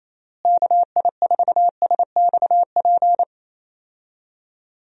Morse code, heard as a keyed tone, about 28 wpm, 700 Hz. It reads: KI4SXP
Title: USA Call Signs - 6 Characters - Repeated 3x - 28wpm